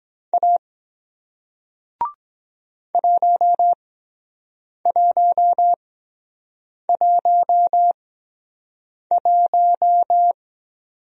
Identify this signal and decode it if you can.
Morse code, heard as a keyed tone, about 23 wpm, spaced as for 7 wpm, 700 Hz. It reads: A 1111